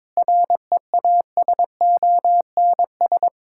Morse code, heard as a keyed tone, about 22 wpm, 700 Hz. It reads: REASONS
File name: practice / 22wpm / Sets of 8 Words - 22wpm